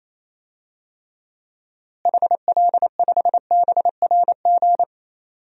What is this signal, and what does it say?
Morse code, heard as a keyed tone, about 28 wpm, 700 Hz. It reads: HL5BRG
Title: DXCC-100 Call Signs - 6 Characters - Repeated 3x - 28wpm